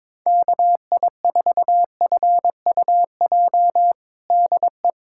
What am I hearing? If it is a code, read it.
Morse code, heard as a keyed tone, about 22 wpm, 700 Hz. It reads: KI4FUJ DE
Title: USA Callsign DE Callsign - 6 Characters - 22wpm